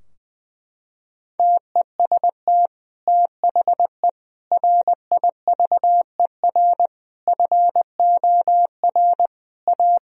Morse code, heard as a keyed tone, about 20 wpm, 700 Hz. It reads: TEST THE RIVER FOR A
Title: Sentences from Top 1000 Words - Review Part 4 - 20wpm